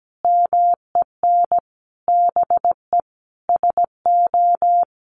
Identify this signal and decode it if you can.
Morse code, heard as a keyed tone, about 17 wpm, 700 Hz. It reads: MEN BE SO
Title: Sentences from Top 1400 Words 17wpm